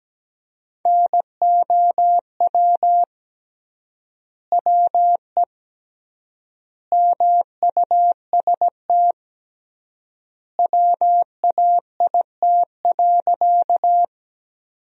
Morse code, heard as a keyed tone, about 17 wpm, 700 Hz. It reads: NOW WE MUST WAIT.
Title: 3x Word-Spacing - Sentences from Top 500 Words - 17wpm